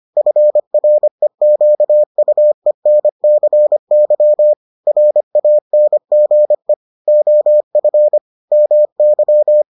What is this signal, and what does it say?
Morse code, heard as a keyed tone, about 25 wpm, 600 Hz. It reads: FREQUENCY RANGE OF MY